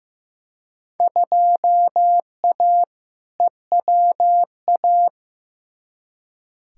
Morse code, heard as a keyed tone, about 15 wpm, 700 Hz. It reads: 2A EWA
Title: Field Day Exchanges - 15wpm